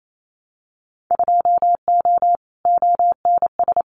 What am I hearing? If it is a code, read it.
Morse code, heard as a keyed tone, about 28 wpm, 690 Hz. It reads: 2O ONS
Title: Winter Field Day Exchanges Encore 28wpm